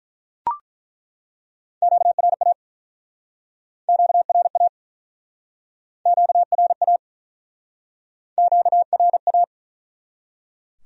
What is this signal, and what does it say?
Morse code, heard as a keyed tone, about 47 wpm, 700 Hz. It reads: QRA QRA QRA QRA